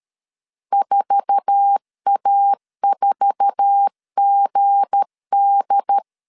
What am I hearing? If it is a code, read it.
Morse code, heard as a keyed tone, about 13 wpm, 800 Hz. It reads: 4A4GD
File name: LEZIONI / 8 YJ45 / 8 YJ4 12wpm 2151